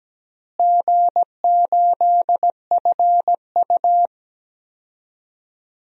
Morse code, heard as a keyed tone, about 17 wpm, 700 Hz. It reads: G8FU
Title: DXCC-100 Call Signs - 4 Characters - Repeated 2x - 17wpm